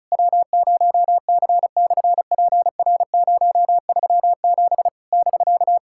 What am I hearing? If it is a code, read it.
Morse code, heard as a keyed tone, about 36 wpm, 700 Hz. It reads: W0C/PR037 <BK>
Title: SOTA QSO Practice 35wpm